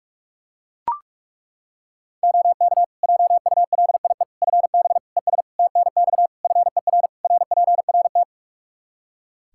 Morse code, heard as a keyed tone, about 41 wpm, 700 Hz. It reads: OK JULIE FB ES TNX FER RPRT
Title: Practice QSO - Exchange 2 - Info - 45wpm